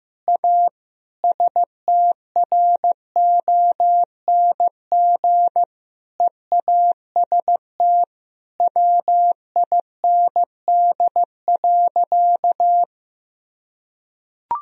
Morse code, heard as a keyed tone, about 15 wpm, 700 Hz. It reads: A STRONG EAST WIND.
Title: Sentences from Top 1000 Words - Review Part 3 - 15wpm